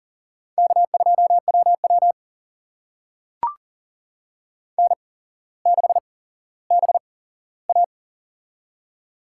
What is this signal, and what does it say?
Morse code, heard as a keyed tone, about 40 wpm, spaced as for 17 wpm, 700 Hz. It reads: K2WW N6BA
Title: USA Call Signs - 4 Characters - 8x Character Spacing - 40wpm